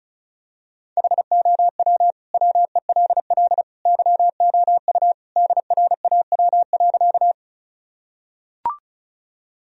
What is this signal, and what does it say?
Morse code, heard as a keyed tone, about 34 wpm, 700 Hz. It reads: HOW WELL YOU DRAW.
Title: Sentences from Top 1000 Words - Review Part 1 - 35wpm